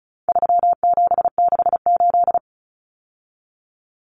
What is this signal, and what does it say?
Morse code, heard as a keyed tone, about 34 wpm, 700 Hz. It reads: 3768
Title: Numbers - 4-Digit 35wpm